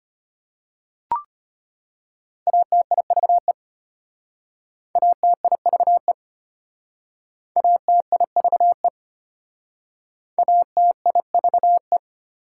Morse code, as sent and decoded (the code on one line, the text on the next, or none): .- - .. ...- . / .- - .. ...- . / .- - .. ...- . / .- - .. ...- .
ATIVE ATIVE ATIVE ATIVE